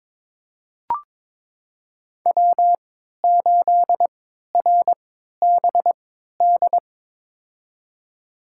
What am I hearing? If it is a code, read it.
Morse code, heard as a keyed tone, about 22 wpm, spaced as for 13 wpm, 700 Hz. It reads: W8RBD
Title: USA Call Signs - 5 Characters - 3x Character Spacing - 22wpm